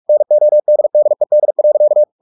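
Morse code, heard as a keyed tone, about 44 wpm, 600 Hz. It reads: NODDED.
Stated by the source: Morse practice at 45 wpm, the novel Never by Ken Follett in English, from CW med Gnister.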